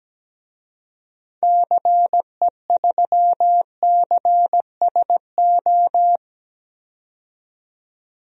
Morse code, heard as a keyed tone, about 17 wpm, 700 Hz. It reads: CE3CSO
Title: DXCC-100 Call Signs - 6 Characters - 17wpm